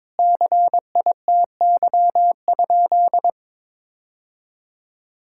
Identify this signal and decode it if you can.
Morse code, heard as a keyed tone, about 21 wpm, 700 Hz. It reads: CITY?
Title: Sentences from Top 1000 Words - Review Part 5 - 22wpm